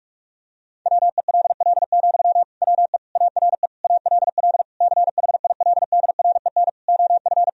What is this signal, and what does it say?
Morse code, heard as a keyed tone, about 41 wpm, 700 Hz. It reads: WELL, WE ARE ALL CHILDREN OF
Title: Sentences from Top 1000 Words - Review Part 4 - 45wpm